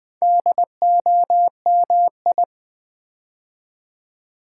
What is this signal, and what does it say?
Morse code, heard as a keyed tone, about 20 wpm, 700 Hz. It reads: DOMI
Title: Build A Word - 2 Letters At A Time - 8-Letter Words - 20wpm